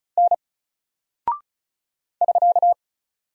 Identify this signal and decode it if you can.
Morse code, heard as a keyed tone, about 36 wpm, 700 Hz. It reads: N <SK>
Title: Single Character (Letter:Number:Punctuation:Prosign) 35wpm